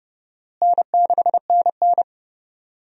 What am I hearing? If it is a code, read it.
Morse code, heard as a keyed tone, about 30 wpm, 700 Hz. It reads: N6NN
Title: USA Call Signs - 4 Characters - Repeated 2x - 30wpm